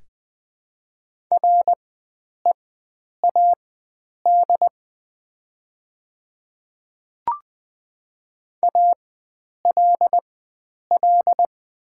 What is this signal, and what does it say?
Morse code, heard as a keyed tone, about 20 wpm, spaced as for 9 wpm, 700 Hz. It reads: READ ALL
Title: 4x Character Spacing - Top 300 Words - 20wpm